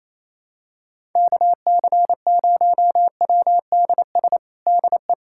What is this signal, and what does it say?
Morse code, heard as a keyed tone, about 28 wpm, 700 Hz. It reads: KC0WDS DE
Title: USA Callsign DE Callsign - 6 Characters - 28wpm